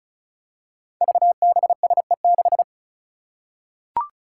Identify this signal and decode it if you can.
Morse code, heard as a keyed tone, about 34 wpm, 700 Hz. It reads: VBSE6
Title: Five Challenging Letter-Number Combinations 35wpm